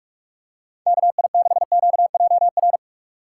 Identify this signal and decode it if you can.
Morse code, heard as a keyed tone, about 46 wpm, 700 Hz. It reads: KI6QJR
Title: USA Call Signs - 6 Characters - 2x Character Spacing - 45wpm